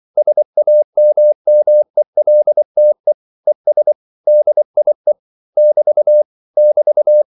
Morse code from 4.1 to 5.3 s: DIE